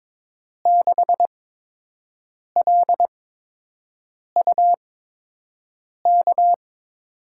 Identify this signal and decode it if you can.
Morse code, heard as a keyed tone, about 22 wpm, spaced as for 6 wpm, 700 Hz. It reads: 6LUK
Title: USA Call Signs - 8x Character Spacing - 22wpm